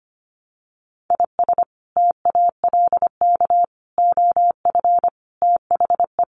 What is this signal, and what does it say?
Morse code, heard as a keyed tone, about 25 wpm, 700 Hz. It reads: IS TALK OF THE
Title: Sets of 5 Words 25wpm